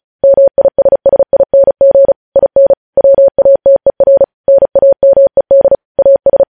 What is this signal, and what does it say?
Morse code, heard as a keyed tone, about 36 wpm, 550 Hz. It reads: MISSING IN WATER NAMED AS
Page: news headlines in morse code at 35 wpm